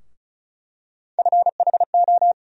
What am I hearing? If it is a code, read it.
Morse code, heard as a keyed tone, about 34 wpm, 700 Hz. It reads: FHO
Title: USA Call Sign Postfixes - 3 Letters 35wpm